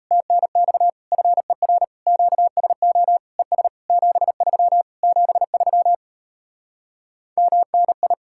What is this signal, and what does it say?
Morse code, heard as a keyed tone, about 38 wpm, 700 Hz. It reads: TNX FER QSO ES 73 73 MNI